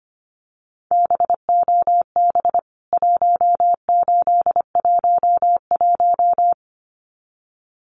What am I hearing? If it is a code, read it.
Morse code, heard as a keyed tone, about 25 wpm, 700 Hz. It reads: BOB 1811
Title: CWT Exchange 25wpm